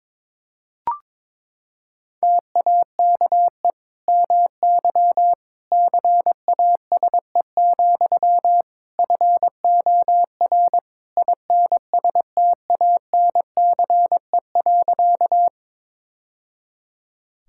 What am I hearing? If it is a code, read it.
Morse code, heard as a keyed tone, about 22 wpm, 700 Hz. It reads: TAKE MY CASE, FOR INSTANCE.
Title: Sentences from Top 1000 Words - Review Part 5 - 22wpm